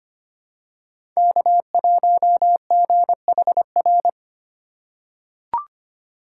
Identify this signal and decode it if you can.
Morse code, heard as a keyed tone, about 25 wpm, 700 Hz. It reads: K1GHR
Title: USA Call Signs - 5 Characters - 6x Character Spacing - 25wpm